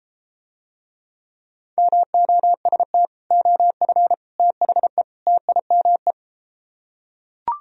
Morse code, heard as a keyed tone, about 31 wpm, 700 Hz. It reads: MOST OF THE TIME